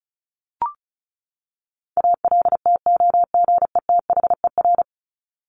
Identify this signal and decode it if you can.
Morse code, heard as a keyed tone, about 34 wpm, 700 Hz. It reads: ALTOGETHER
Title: Top 1501-1750 Words 35wpm